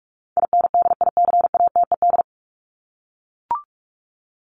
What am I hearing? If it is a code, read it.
Morse code, heard as a keyed tone, about 44 wpm, 700 Hz. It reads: INDICATED
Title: ED Suffix - Top 100 Words with Base Words 45wpm